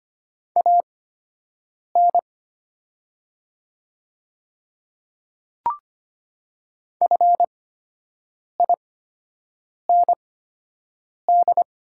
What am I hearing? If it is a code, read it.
Morse code, heard as a keyed tone, about 25 wpm, spaced as for 7 wpm, 700 Hz. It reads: AN FIND
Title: 8x Character Spacing - Top 100 Words - 25wpm